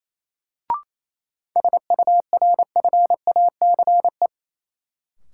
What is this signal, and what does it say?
Morse code, heard as a keyed tone, about 28 wpm, 700 Hz. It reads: SURFACE
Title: Sentences from Top 600 Words - New Word Drill - 28wpm